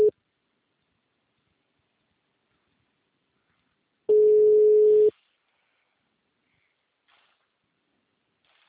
Ringback tone (Europe, 425 Hz).
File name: out- -2010- - - .69